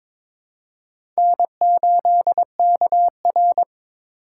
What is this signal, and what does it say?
Morse code, heard as a keyed tone, about 22 wpm, 700 Hz. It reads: N8KR
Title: USA Call Signs - 4 Characters - 4x Character Spacing - 22wpm